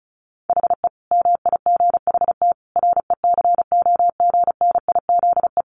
Morse code, heard as a keyed tone, about 34 wpm, 700 Hz. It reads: HE MIGHT RECOGNIZE
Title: Sentences from Top 1400 Words 35wpm